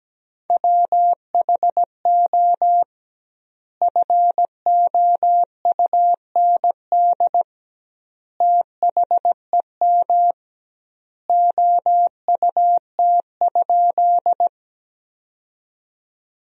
Morse code, heard as a keyed tone, about 17 wpm, 700 Hz. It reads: WHO FOUND THEM OUT?
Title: 2x Word-Spacing - Sentences from Top 300 Words - 17wpm